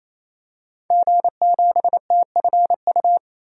Morse code, heard as a keyed tone, about 28 wpm, 700 Hz. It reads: G7TFU
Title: DXCC-100 Call Signs - 5 Characters - 28wpm